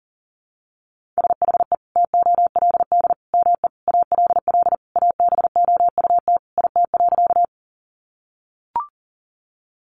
Morse code, heard as a keyed tone, about 40 wpm, 700 Hz. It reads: SHE TOLD ME ALL ABOUT IT.
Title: Sentences from Top 400 Words 40wpm